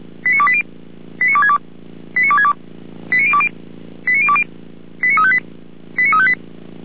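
Five-tone selective call (ZVEI), heard as quick runs of five tones, tones of about 70 ms. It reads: 89280 89262 89262 89020 89020 89358 89358
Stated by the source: ZVEI five-tone selective call received on an SDR.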